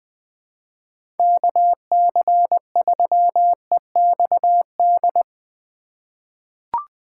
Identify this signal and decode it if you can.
Morse code, heard as a keyed tone, about 20 wpm, 700 Hz. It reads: KC3EXD
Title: USA Call Signs - 3x Character Spacing - 20wpm